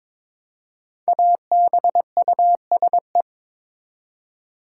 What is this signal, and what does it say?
Morse code, heard as a keyed tone, about 22 wpm, 700 Hz. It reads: ABUSE
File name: 5 Letter Words - First 3 Letters Given - 22wpm